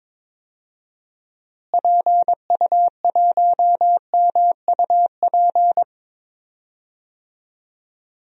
Morse code, heard as a keyed tone, about 21 wpm, 700 Hz. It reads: PU1MUP
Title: DXCC-100 Call Signs - 6 Characters - Repeated 2x - 22wpm